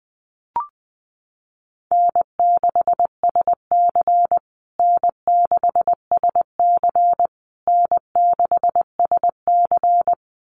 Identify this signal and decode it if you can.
Morse code, heard as a keyed tone, about 20 wpm, 700 Hz. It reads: N6SC N6SC N6SC
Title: USA Call Signs Repeated 3x 20wpm